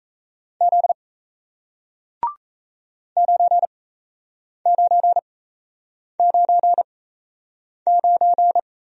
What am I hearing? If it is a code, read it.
Morse code, heard as a keyed tone, about 36 wpm, spaced as for 8 wpm, 700 Hz. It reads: Z 9999